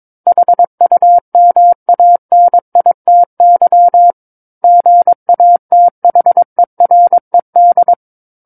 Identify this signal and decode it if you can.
Morse code, heard as a keyed tone, about 22 wpm, 700 Hz. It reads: HUMANITY GATHERED
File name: Book One / 13-Chapter 13 - 22 WPM